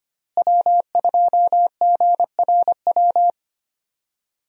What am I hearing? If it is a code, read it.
Morse code, heard as a keyed tone, about 25 wpm, 700 Hz. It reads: W2GRW
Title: USA Call Signs - 5 Characters - Repeated 3x - 25wpm